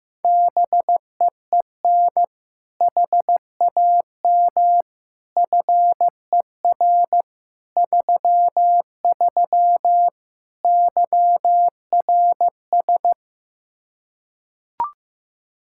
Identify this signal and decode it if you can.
Morse code, heard as a keyed tone, about 16 wpm, 700 Hz. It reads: BEEN HAM FER 33 YRS
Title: Practice QSO - Exchange 3 - More Info - 15wpm